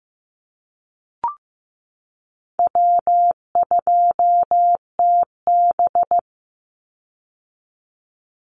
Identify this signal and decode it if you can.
Morse code, heard as a keyed tone, about 15 wpm, 700 Hz. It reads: W2TB
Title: Easy Call Signs - 15wpm